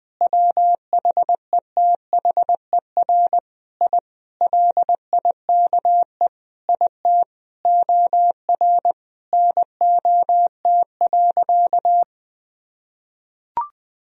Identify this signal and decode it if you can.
Morse code, heard as a keyed tone, about 20 wpm, 700 Hz. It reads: WHETHER I LIKE IT OR NOT.
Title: Sentences from Top 1000 Words - Review Part 3 - 20wpm